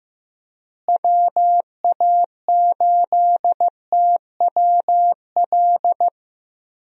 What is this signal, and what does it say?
Morse code, heard as a keyed tone, about 15 wpm, 700 Hz. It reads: WA8TWL
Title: USA Call Signs - 6 Characters - Repeated 3x - 15wpm